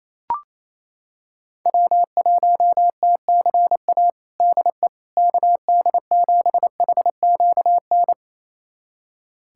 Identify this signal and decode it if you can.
Morse code, heard as a keyed tone, about 27 wpm, 700 Hz. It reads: W1TCA DE KD7HQN
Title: USA Callsign DE Callsign 28wpm